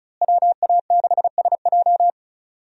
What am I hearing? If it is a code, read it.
Morse code, heard as a keyed tone, about 34 wpm, 700 Hz. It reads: WA6SJ
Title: USA Call Signs - 5 Characters - Repeated 2x - 35wpm